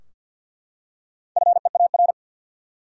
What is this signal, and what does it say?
Morse code, heard as a keyed tone, about 51 wpm, 700 Hz. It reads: FEAR
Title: Four Letter Words - 50wpm